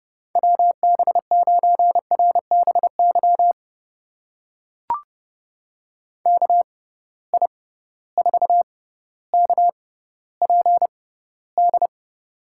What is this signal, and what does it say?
Morse code, heard as a keyed tone, about 30 wpm, spaced as for 15 wpm, 700 Hz. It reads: WB9RBY KI4KPD
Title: USA Call Signs - 6 Characters - 6x Character Spacing - 30wpm